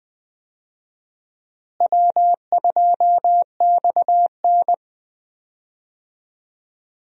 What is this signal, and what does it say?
Morse code, heard as a keyed tone, about 20 wpm, 700 Hz. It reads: W2XN
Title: USA Call Signs - 4 Characters - Repeated 2x - 20wpm